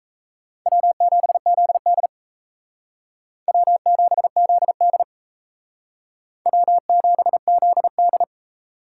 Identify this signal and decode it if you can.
Morse code, heard as a keyed tone, about 36 wpm, 700 Hz. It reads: W7ZD W7ZD W7ZD